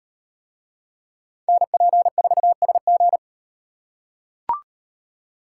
Morse code, heard as a keyed tone, about 38 wpm, 700 Hz. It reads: NP4SG